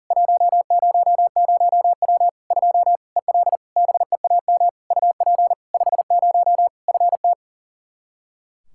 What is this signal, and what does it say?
Morse code, heard as a keyed tone, about 40 wpm, 700 Hz. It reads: 100W 2 EL BEAM UP 50 FT
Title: Practice QSO - Exchange 2 - Info - 40wpm